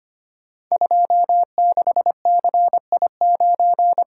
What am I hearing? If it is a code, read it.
Morse code, heard as a keyed tone, about 25 wpm, 700 Hz. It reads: 26CI9
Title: Code Groups - 5 Letter:Numbers - 25wpm